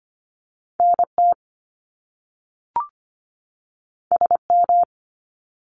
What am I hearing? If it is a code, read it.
Morse code, heard as a keyed tone, about 25 wpm, 700 Hz. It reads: NT SM
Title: Top Two Letter Combinations in Words 25wpm